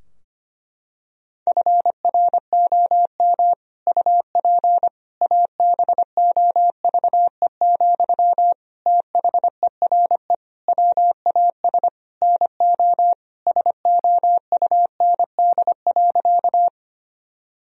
Morse code, heard as a keyed tone, about 25 wpm, 700 Hz. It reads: FROM UP ABOVE, THERE WAS NO SOUND.
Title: Sentences from Top 1000 Words - Review Part 2 - 25wpm